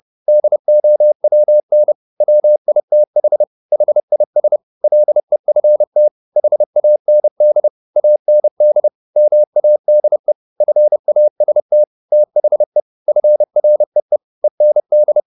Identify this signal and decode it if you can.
Morse code, heard as a keyed tone, about 30 wpm, 600 Hz. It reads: DOWN WITH HIS LEFT HAND AND MADE FAST THE FREE END
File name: the old man and the sea / the old man and the sea 1 30wpm